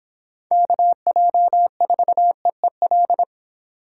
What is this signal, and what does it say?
Morse code, heard as a keyed tone, about 24 wpm, 700 Hz. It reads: KJ4EEL